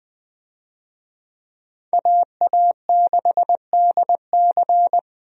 Morse code, heard as a keyed tone, about 20 wpm, 700 Hz. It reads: AA6DC
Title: USA Call Signs - 5 Characters - 20wpm